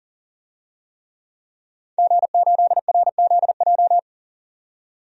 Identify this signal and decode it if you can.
Morse code, heard as a keyed tone, about 40 wpm, 700 Hz. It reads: G8RZJ
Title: DXCC-100 Call Signs - 5 Characters - 40wpm